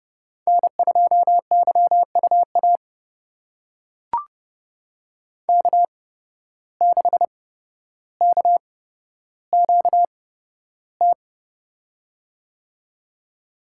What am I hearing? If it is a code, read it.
Morse code, heard as a keyed tone, about 30 wpm, spaced as for 13 wpm, 700 Hz. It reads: N2YUA K6KQT
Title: USA Call Signs - 5 Characters - 8x Character Spacing - 30wpm